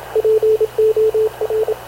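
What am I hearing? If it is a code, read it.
Morse code, heard as a keyed tone, about 27 wpm, 440 Hz, with radio noise. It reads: POR
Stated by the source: Havana Radio CLA press (PX) broadcast, recorded by Radio Officer Paul Dumesnil aboard mv CNAV QUEST/CZDO.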